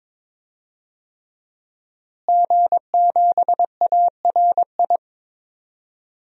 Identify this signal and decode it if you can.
Morse code, heard as a keyed tone, about 22 wpm, 700 Hz. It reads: G7ARI
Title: DXCC-100 Call Signs - 5 Characters - Repeated 3x - 22wpm